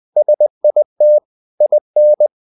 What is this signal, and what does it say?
Morse code, heard as a keyed tone, about 20 wpm, 600 Hz. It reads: SIT IN
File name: the old man and the sea / the old man and the sea part 1 20wpm